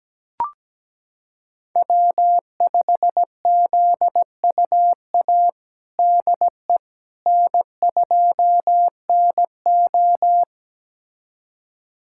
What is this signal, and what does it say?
Morse code, heard as a keyed tone, about 17 wpm, 700 Hz. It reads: W5ZUA DE N2NO
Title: USA Callsign DE Callsign 17wpm